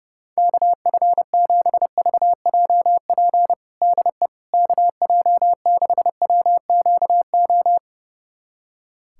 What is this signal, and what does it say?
Morse code, heard as a keyed tone, about 32 wpm, 700 Hz. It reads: KF7VJP DE KJ6WQO
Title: USA Callsign DE Callsign 30wpm